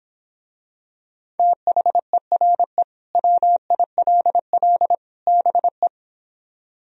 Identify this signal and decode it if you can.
Morse code, heard as a keyed tone, about 26 wpm, 700 Hz. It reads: THERE WILL BE
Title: Speed-Racing - Sets of 3 Words 22wpm